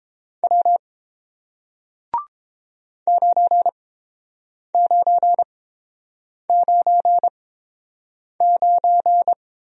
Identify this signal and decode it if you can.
Morse code, heard as a keyed tone, about 30 wpm, spaced as for 7 wpm, 700 Hz. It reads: W 9999